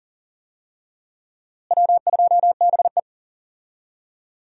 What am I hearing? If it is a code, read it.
Morse code, heard as a keyed tone, about 40 wpm, 700 Hz. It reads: W2BE
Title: USA Call Signs - 4 Characters - Repeated 2x - 40wpm